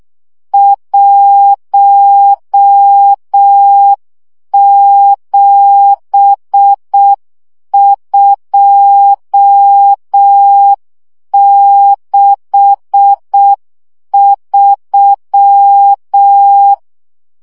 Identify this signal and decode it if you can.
Morse code, heard as a keyed tone, about 6 wpm, 800 Hz. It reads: 17263
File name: morze vizsga 2011-3 5